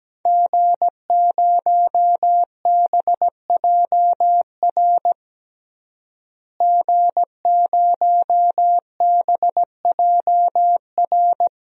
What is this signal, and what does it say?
Morse code, heard as a keyed tone, about 17 wpm, 700 Hz. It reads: G0BJR G0BJR
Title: DXCC-100 Call Signs - 5 Characters - Repeated 2x - 17wpm